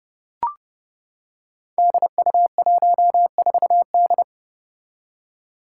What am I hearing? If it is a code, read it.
Morse code, heard as a keyed tone, about 30 wpm, 700 Hz. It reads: DU14D
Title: Five Challenging Letter-Number Combinations 30wpm